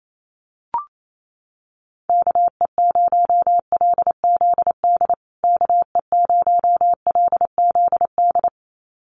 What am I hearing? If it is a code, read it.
Morse code, heard as a keyed tone, about 28 wpm, 700 Hz. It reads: KE0LZD KE0LZD
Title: USA Call Signs Repeated 2x 28wpm